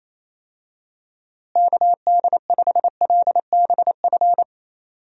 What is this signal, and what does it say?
Morse code, heard as a keyed tone, about 27 wpm, 700 Hz. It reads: KD5LBF
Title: USA Call Signs - 6 Characters - Repeated 3x - 28wpm